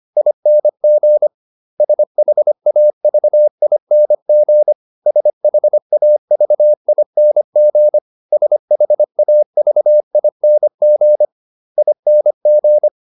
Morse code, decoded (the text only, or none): ING SHAVING SHAVING SHAVING ING